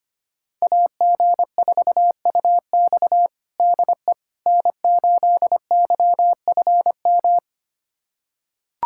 Morse code, heard as a keyed tone, about 25 wpm, 700 Hz. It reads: AG4UX DE N8YFM
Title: USA Callsign DE Callsign - 5 Characters - 25wpm